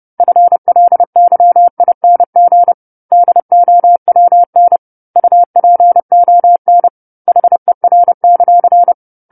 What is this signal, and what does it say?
Morse code, heard as a keyed tone, about 30 wpm, 700 Hz. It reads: FLYING DOWN UPON HER;
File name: 10-Chapter-10 30 WPM